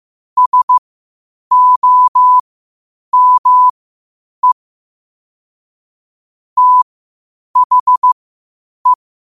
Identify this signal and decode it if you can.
Morse code, heard as a keyed tone, about 15 wpm, spaced as for 8 wpm, 1000 Hz. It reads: SOME THE
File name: qotdfiles / 8wpm